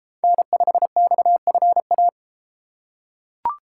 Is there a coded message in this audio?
Yes: Morse code from 0.2 to 2.1 s, heard as a keyed tone, about 30 wpm, 700 Hz. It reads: N5XFA